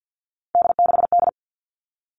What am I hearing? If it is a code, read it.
Morse code, heard as a keyed tone, about 50 wpm, 700 Hz. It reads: D6D
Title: All Cut Numbers - 3-Digit 50wpm